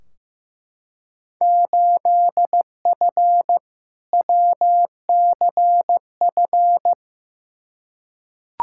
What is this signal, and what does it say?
Morse code, heard as a keyed tone, about 15 wpm, 700 Hz. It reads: 8F WCF
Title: Field Day Exchanges Encore - 15wpm